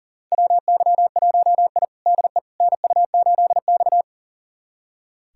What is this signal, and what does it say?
Morse code, heard as a keyed tone, about 40 wpm, 700 Hz. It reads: WY1I DE NU8X
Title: USA Callsign DE Callsign - 4 Characters - 40wpm